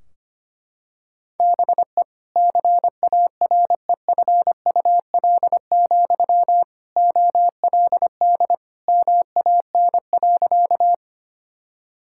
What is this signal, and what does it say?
Morse code, heard as a keyed tone, about 25 wpm, 700 Hz. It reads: BE CAREFUL, OLD MAN.
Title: Sentences from Top 1600 Words 25wpm